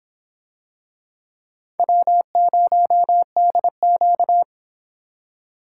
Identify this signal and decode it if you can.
Morse code, heard as a keyed tone, about 24 wpm, 700 Hz. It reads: W0DQ